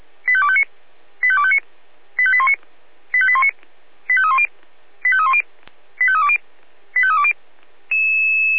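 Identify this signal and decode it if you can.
Five-tone selective call (ZVEI), heard as quick runs of five tones, tones of about 70 ms. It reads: 85259 85259 85819 85819 85310 85310 85320 85320
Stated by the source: ZVEI five-tone selective call received on an SDR.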